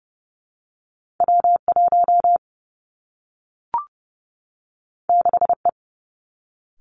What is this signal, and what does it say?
Morse code, heard as a keyed tone, about 30 wpm, 700 Hz. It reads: W 1 6 E
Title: Two Challenging Letter-Number Combinations 30wpm